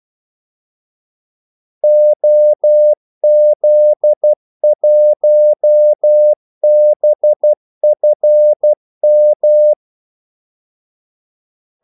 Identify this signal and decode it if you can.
Morse code, heard as a keyed tone, about 12 wpm, 600 Hz. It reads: OZ1BFM
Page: 2 ekstra ordmellemrum mellem kaldesignalerne